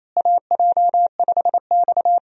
Morse code, heard as a keyed tone, about 28 wpm, 700 Hz. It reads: AJ5X
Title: USA Call Signs - 4 Characters - Repeated 2x - 28wpm